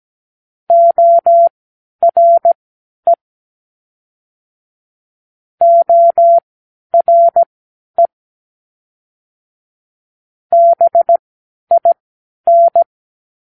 Morse code, heard as a keyed tone, about 17 wpm, spaced as for 10 wpm, 680 Hz. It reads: ORE ORE BIN
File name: k7qo / Practicewords